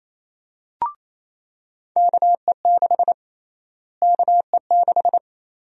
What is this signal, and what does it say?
Morse code, heard as a keyed tone, about 28 wpm, 700 Hz. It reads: KE6 KE6
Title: USA Call Sign Prefixes - Repeated 2x - 28wpm